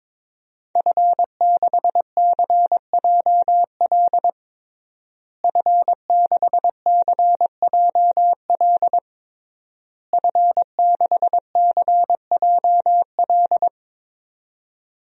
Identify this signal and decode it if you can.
Morse code, heard as a keyed tone, about 22 wpm, 700 Hz. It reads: F6CJL F6CJL F6CJL
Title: DXCC-100 Call Signs - 5 Characters - Repeated 3x - 22wpm